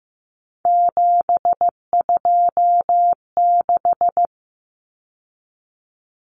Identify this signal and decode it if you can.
Morse code, heard as a keyed tone, about 15 wpm, 700 Hz. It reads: 726
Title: Common Cut Numbers - 3-Digit 15wpm